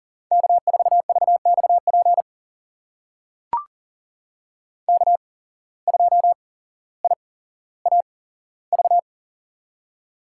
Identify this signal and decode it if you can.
Morse code, heard as a keyed tone, about 40 wpm, spaced as for 17 wpm, 700 Hz. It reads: K4VXP K2IAV
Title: USA Call Signs - 5 Characters - 8x Character Spacing - 40wpm